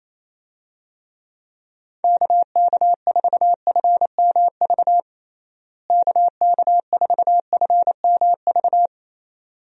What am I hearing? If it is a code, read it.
Morse code, heard as a keyed tone, about 28 wpm, 700 Hz. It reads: KK4FMV KK4FMV
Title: USA Call Signs - 6 Characters - Repeated 2x - 28wpm